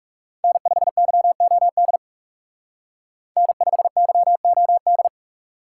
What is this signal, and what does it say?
Morse code, heard as a keyed tone, about 38 wpm, 700 Hz. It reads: N5YOD N5YOD